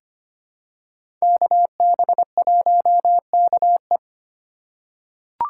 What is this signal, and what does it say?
Morse code, heard as a keyed tone, about 25 wpm, 700 Hz. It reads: KB1KE
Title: USA Call Signs - 5 Characters - 2x Character Spacing - 25wpm